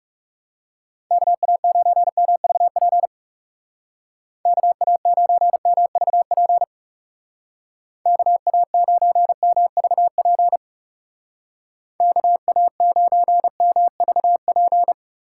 Morse code, heard as a keyed tone, about 38 wpm, 700 Hz. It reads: KA9MVP KA9MVP KA9MVP KA9MVP